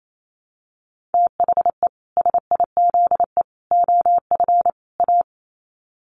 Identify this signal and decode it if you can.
Morse code, heard as a keyed tone, about 28 wpm, 700 Hz. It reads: THE SIZE OF A
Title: Sets of 4 Words 28wpm